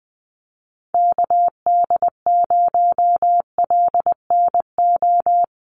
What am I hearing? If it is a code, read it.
Morse code, heard as a keyed tone, about 20 wpm, 700 Hz. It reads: KD0LNO
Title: USA Call Signs Repeated 3x 20wpm